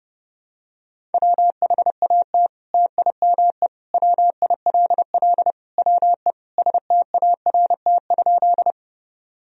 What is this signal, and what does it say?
Morse code, heard as a keyed tone, about 30 wpm, 700 Hz. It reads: WHAT TIME WILL WE START?
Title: Sentences from Top 1000 Words - Review Part 1 - 30wpm